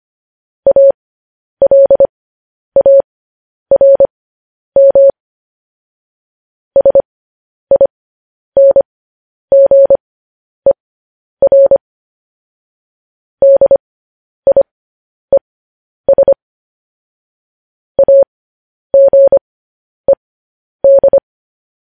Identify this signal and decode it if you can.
Morse code, heard as a keyed tone, about 25 wpm, spaced as for 10 wpm, 550 Hz. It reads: ALARM SINGER DIES AGED
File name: podcast / News headlines-1x-25x10